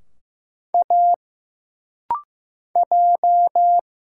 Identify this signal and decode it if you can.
Morse code, heard as a keyed tone, about 15 wpm, 700 Hz. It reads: A J
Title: Single Letters - Rapid-Fire 15wpm